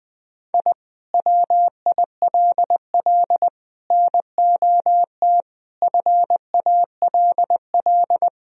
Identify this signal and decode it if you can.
Morse code, heard as a keyed tone, about 20 wpm, 700 Hz. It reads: I WILL NOT FALL
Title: Sentences from Top 1000 Words - Review Part 2 - 20wpm